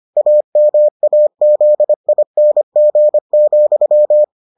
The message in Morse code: AMAZING,

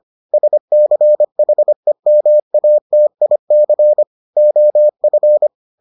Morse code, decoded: SCHEMATIC OF

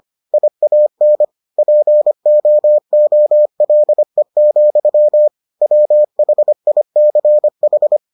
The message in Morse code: IAN POOLE, WHICH

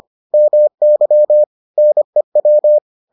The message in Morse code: MY NEW

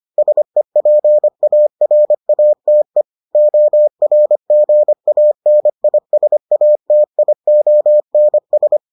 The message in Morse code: SEPARATE ORGANISATIONS

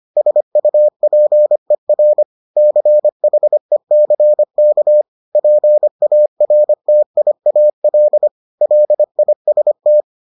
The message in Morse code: SUPER CHECK PARTIAL LIST

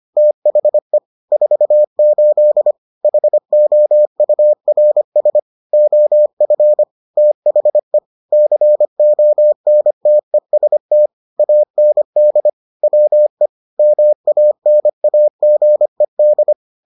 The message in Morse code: THE 48 HOURS OF THE CONTEST AND WE MANAGED

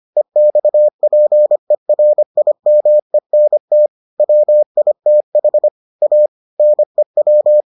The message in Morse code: EXPERIMENT WITH A NEW

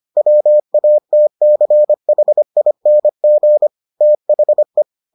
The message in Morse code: WATCHING THE

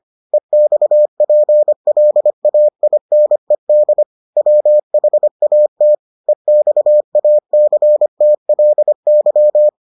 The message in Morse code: EXPLAINED WHAT EXACTLY

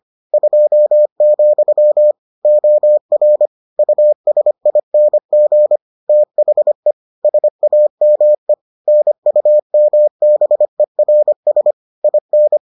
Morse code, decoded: 2, OR USING THE SAME NUMBERS IN

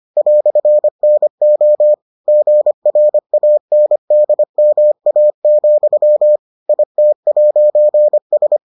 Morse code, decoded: "NO GRANDMA, IT'S

